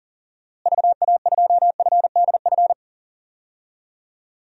Morse code, heard as a keyed tone, about 40 wpm, 700 Hz. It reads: VA2FDF
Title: DXCC-100 Call Signs - 6 Characters - Repeated 2x - 40wpm